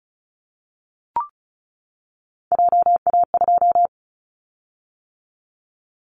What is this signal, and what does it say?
Morse code, heard as a keyed tone, about 36 wpm, 690 Hz. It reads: JA2
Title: Callsign Prefixes for Top 25 DXCC - 35wpm